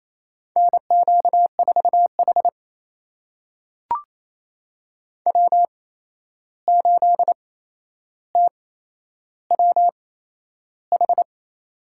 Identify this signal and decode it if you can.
Morse code, heard as a keyed tone, about 28 wpm, spaced as for 11 wpm, 700 Hz. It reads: NQ4H W8TWH
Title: USA Call Signs - 8x Character Spacing - 28wpm